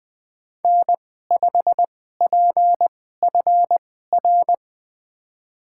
Morse code, heard as a keyed tone, about 20 wpm, spaced as for 15 wpm, 700 Hz. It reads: N5PFR
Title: USA Call Signs - 5 Characters - 2x Character Spacing - 20wpm